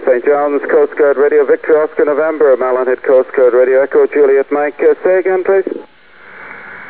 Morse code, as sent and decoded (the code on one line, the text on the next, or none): none